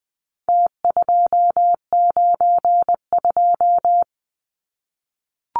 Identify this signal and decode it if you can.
Morse code, heard as a keyed tone, about 20 wpm, 700 Hz. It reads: T292
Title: Numbers - Contest Style - 4-Digit 20wpm